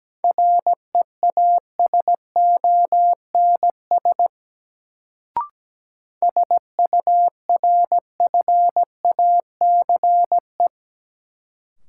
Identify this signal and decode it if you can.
Morse code, heard as a keyed tone, about 17 wpm, 700 Hz. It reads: REASONS SURFACE
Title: Sets of 4 Words - New Word Drill - 17wpm